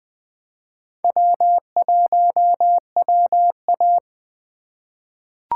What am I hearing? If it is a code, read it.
Morse code, heard as a keyed tone, about 20 wpm, 700 Hz. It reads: W1WA
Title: USA Call Signs - 4 Characters - 2x Character Spacing - 20wpm